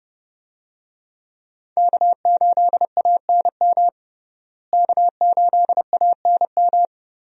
Morse code, heard as a keyed tone, about 30 wpm, 700 Hz. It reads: K8ANM K8ANM
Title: USA Call Signs - 5 Characters - Repeated 3x - 30wpm